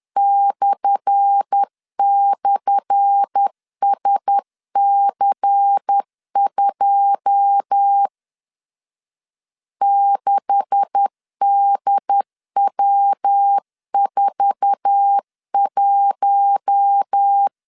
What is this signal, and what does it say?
Morse code, heard as a keyed tone, about 10 wpm, 800 Hz. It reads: //SC2 6DW41
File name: LEZIONI / 10 SIGNS / 10A SLASH 10wpm 2808